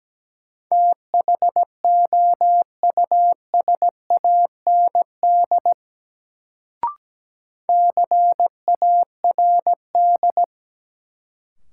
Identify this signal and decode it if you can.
Morse code, heard as a keyed tone, about 17 wpm, 700 Hz. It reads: THOUSAND CARD
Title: Sentences from Top 600 Words - New Word Drill - 17wpm